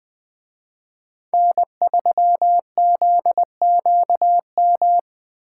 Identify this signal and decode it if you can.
Morse code, heard as a keyed tone, about 23 wpm, 700 Hz. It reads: N3ZQM
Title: Speed Racing - USA 1x3 Call Signs 20wpm